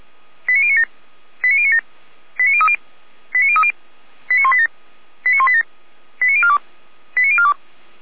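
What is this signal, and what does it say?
Five-tone selective call (ZVEI), heard as quick runs of five tones, tones of about 70 ms. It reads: 89097 89097 89030 89030 89187 89187 89042 89042
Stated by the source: ZVEI five-tone call, code 89229, received by SDR.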